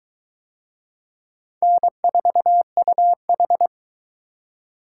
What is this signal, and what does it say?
Morse code, heard as a keyed tone, about 22 wpm, 700 Hz. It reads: N4UH